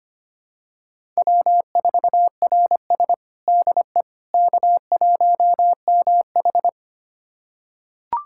Morse code, heard as a keyed tone, about 25 wpm, 700 Hz. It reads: W4RS DE K1MH
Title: USA Callsign DE Callsign - 4 Characters - 25wpm